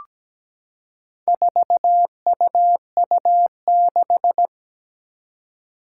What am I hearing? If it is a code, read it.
Morse code, heard as a keyed tone, about 17 wpm, 700 Hz. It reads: 4UU6
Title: Four Challenging Letter-Number Combinations 17wpm